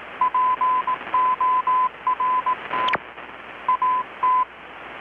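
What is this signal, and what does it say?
Morse code, heard as a keyed tone, about 18 wpm, 1010 Hz. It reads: PORT AT
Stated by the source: Morse from KPH's Night of Nights XX maritime radiotelegraphy event, received in Chiapas, Mexico about 3,800 km away on a Kenwood R-600 with a grounded 10 m long-wire antenna.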